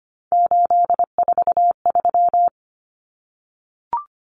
Morse code, heard as a keyed tone, about 25 wpm, 700 Hz. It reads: 843
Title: Common Cut Numbers - 3-Digit 25wpm